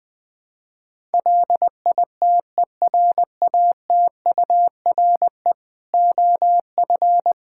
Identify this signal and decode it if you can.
Morse code, heard as a keyed tone, about 20 wpm, 700 Hz. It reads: LITERATURE OF
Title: Sentences from Top 1000 Words - Review Part 5 - 20wpm